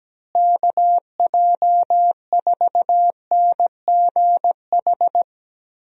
Morse code, heard as a keyed tone, about 17 wpm, 700 Hz. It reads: KJ4NGH
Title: USA Call Signs - 6 Characters - Repeated 2x - 17wpm